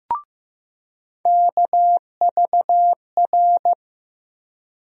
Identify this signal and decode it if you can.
Morse code, heard as a keyed tone, about 15 wpm, 700 Hz. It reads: KVR
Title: USA Call Sign Postfixes - 3 Letters 15wpm